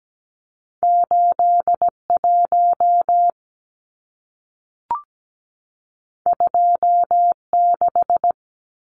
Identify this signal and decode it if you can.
Morse code, heard as a keyed tone, about 17 wpm, 700 Hz. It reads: 81 26
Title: Numbers - 2-Digit 17wpm